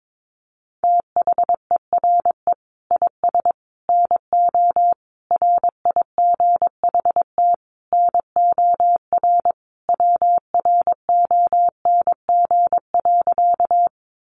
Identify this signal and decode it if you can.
Morse code, heard as a keyed tone, about 22 wpm, 700 Hz. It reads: THERE IS NO RIGHT NOR WRONG.
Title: Sentences from Top 1500 Words 22wpm